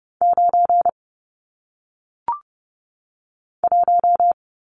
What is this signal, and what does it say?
Morse code, heard as a keyed tone, about 30 wpm, 700 Hz. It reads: 9 1
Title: Numbers - 1-Digit 30wpm